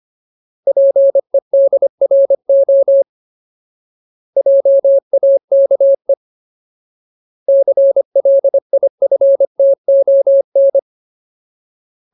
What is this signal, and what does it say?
Morse code, heard as a keyed tone, about 25 wpm, 550 Hz. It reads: PEDRO JAKE CLIFTON
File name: Ultimate Word List / 25WPM Ultimate